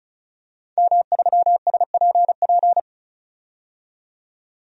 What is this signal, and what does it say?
Morse code, heard as a keyed tone, about 34 wpm, 700 Hz. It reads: M3SPP
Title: DXCC-100 Call Signs - 5 Characters - 35wpm